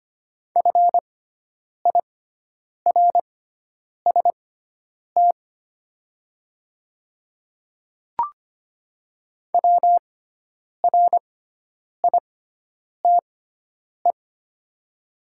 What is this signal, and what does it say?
Morse code, heard as a keyed tone, about 25 wpm, spaced as for 9 wpm, 700 Hz. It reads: FIRST WRITE
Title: 6x Character Spacing - Top 100 Words - 25wpm